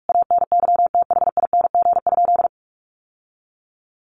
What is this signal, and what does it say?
Morse code, heard as a keyed tone, about 44 wpm, 700 Hz. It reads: ANYTHING?
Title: Sentences from Top 1500 Words 45wpm